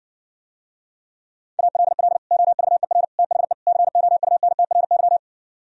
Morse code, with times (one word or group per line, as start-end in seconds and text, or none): 1.6-2.2 ALL
2.3-3.0 OVER
3.2-3.5 THE
3.7-5.2 COUNTRY